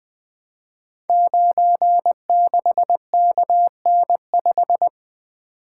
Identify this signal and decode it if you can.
Morse code, heard as a keyed tone, about 20 wpm, 700 Hz. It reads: 96KN5
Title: Code Groups - 5 Letter:Numbers - 20wpm